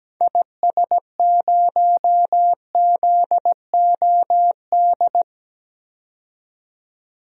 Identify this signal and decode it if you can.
Morse code, heard as a keyed tone, about 17 wpm, 700 Hz. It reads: IS0ZOD
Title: DXCC-100 Call Signs - 6 Characters - 17wpm